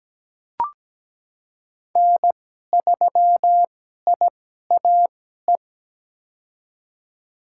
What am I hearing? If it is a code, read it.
Morse code, heard as a keyed tone, about 17 wpm, spaced as for 12 wpm, 700 Hz. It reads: N3IAE
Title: USA Call Signs - 2x Character Spacing - 17wpm